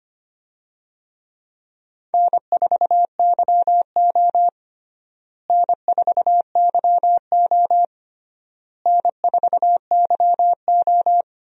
Morse code, heard as a keyed tone, about 25 wpm, 700 Hz. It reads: N4YO N4YO N4YO
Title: USA Call Signs - 4 Characters - Repeated 3x - 25wpm